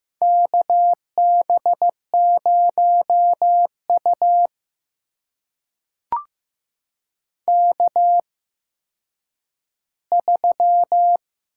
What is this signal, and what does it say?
Morse code, heard as a keyed tone, about 15 wpm, spaced as for 9 wpm, 700 Hz. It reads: KB0U K3
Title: USA Call Signs - 8x Character Spacing - 15wpm